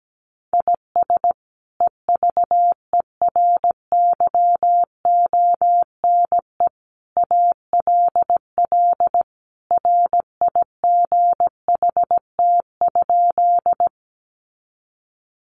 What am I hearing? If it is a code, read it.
Morse code, heard as a keyed tone, about 17 wpm, 700 Hz. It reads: IS EVERYONE ALL RIGHT?
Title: Sentences from Top 800 Words 17wpm